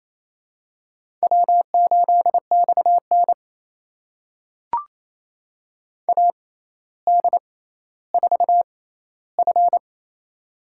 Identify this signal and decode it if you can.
Morse code, heard as a keyed tone, about 28 wpm, spaced as for 14 wpm, 700 Hz. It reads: W8XN AD4F
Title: USA Call Signs - 4 Characters - 6x Character Spacing - 28wpm